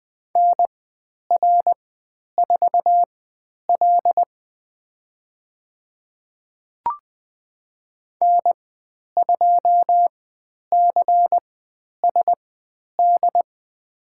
Morse code, heard as a keyed tone, about 20 wpm, spaced as for 10 wpm, 700 Hz. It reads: NR4L N2CSD
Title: 102 Lesson 17g - Letter u - Call Signs 20wpm